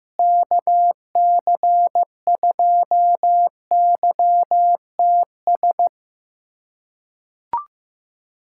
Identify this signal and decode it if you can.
Morse code, heard as a keyed tone, about 15 wpm, 700 Hz. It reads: KC2YTS
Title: USA Callsign DE Callsign - 6 Characters - 15wpm